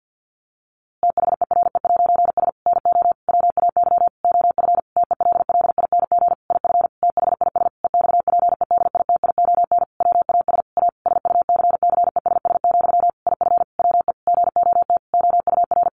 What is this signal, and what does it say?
Morse code, heard as a keyed tone, about 50 wpm, 700 Hz. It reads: THERE'S NO WAY OF TELLING IF THIS EXPEDITION WAS A SUCCESS, IF WE GOT OUR
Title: A Sound Of Thunder 50wpm